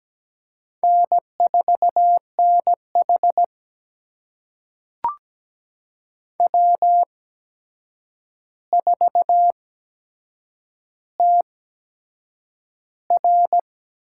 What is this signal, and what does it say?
Morse code, heard as a keyed tone, about 17 wpm, spaced as for 7 wpm, 700 Hz. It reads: N4NH W4TR
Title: USA Call Signs - 4 Characters - 8x Character Spacing - 17wpm